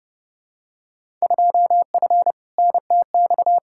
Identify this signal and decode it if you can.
Morse code, heard as a keyed tone, about 32 wpm, 700 Hz. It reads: 2F NTX
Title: Field Day Exchanges - 30wpm